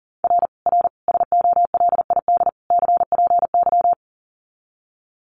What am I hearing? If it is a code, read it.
Morse code, heard as a keyed tone, about 40 wpm, 700 Hz. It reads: R R SOLID CPY
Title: Common QSO Phrases 40wpm